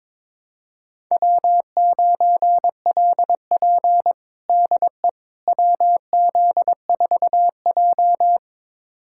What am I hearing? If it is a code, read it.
Morse code, heard as a keyed tone, about 22 wpm, 700 Hz. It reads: W9LP DE WZ4J
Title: USA Callsign DE Callsign - 4 Characters - 22wpm